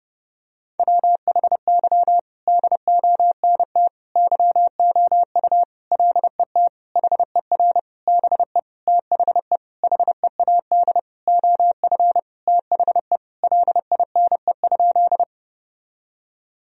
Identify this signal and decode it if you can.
Morse code, heard as a keyed tone, about 30 wpm, 700 Hz. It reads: WHY DONT YOU LET HER BE THE HEAD OF THE LINE?
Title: Sentences from Top 1000 Words - Review Part 1 - 30wpm